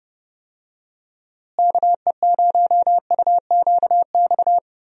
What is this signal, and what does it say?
Morse code, heard as a keyed tone, about 31 wpm, 700 Hz. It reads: KE0UQX
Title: USA Call Signs - 6 Characters - Repeated 3x - 30wpm